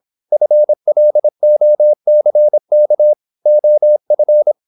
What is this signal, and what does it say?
Morse code, heard as a keyed tone, about 26 wpm, 600 Hz. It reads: FLOCK OF
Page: Never 26wpm | CW med Gnister